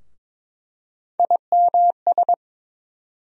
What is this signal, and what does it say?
Morse code, heard as a keyed tone, about 22 wpm, 700 Hz. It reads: IMS
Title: USA Call Sign Postfixes - 3 Letters 22wpm